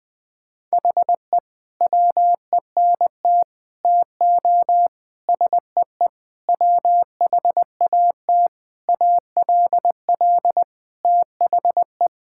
Morse code, heard as a keyed tone, about 21 wpm, 700 Hz. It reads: HE WENT TO SEE WHAT ALL THE
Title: Sentences with 8-Word Phrases - 20wpm